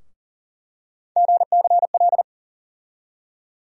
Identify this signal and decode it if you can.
Morse code, heard as a keyed tone, about 40 wpm, 700 Hz. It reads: GCL
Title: USA Call Sign Postfixes - 3 Letters 40wpm